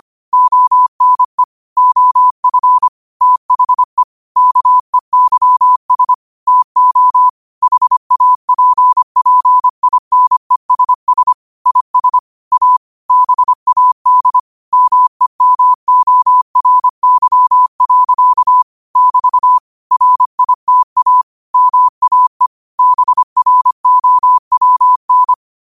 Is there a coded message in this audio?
Yes: Morse code from 0.3 to 25.3 s, heard as a keyed tone, about 25 wpm, 1000 Hz. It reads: ONE OF THE KEYS TO HAPPINESS IS A BAD MEMORY. <BT> RITA MAE BROWN